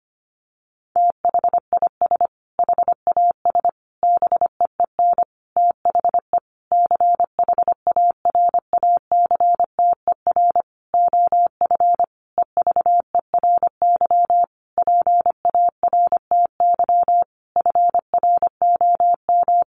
Morse code, heard as a keyed tone, about 25 wpm, 700 Hz. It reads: THIS HAS BEEN THE CHARACTER OF EVERY PARTY FROM
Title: Sentences from Top 1400 Words 25wpm